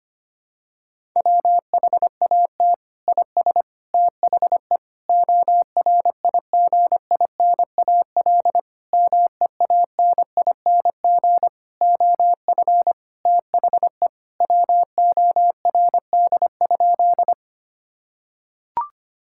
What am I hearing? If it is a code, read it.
Morse code, heard as a keyed tone, about 25 wpm, 700 Hz. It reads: WHAT IS THE ORIGINAL MEANING OF THE WORD?
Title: Sentences from Top 1000 Words - Review Part 6 - 25wpm